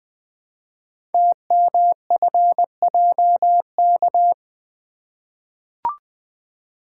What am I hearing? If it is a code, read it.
Morse code, heard as a keyed tone, about 20 wpm, 700 Hz. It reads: TMFJK
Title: Code Groups - 5 Letter:Numbers - 20wpm